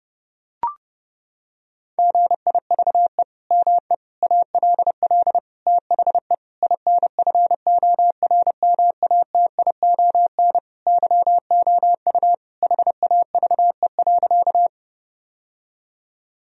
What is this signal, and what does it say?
Morse code, heard as a keyed tone, about 32 wpm, 700 Hz. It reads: GIVE ME ALL THE INFORMATION YOU HAVE.
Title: Sentences from Top 1000 Words - Review Part 3 - 30wpm